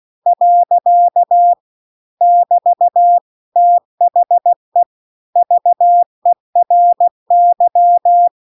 Morse code, heard as a keyed tone, about 16 wpm, 700 Hz. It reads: . <BT> THE VERY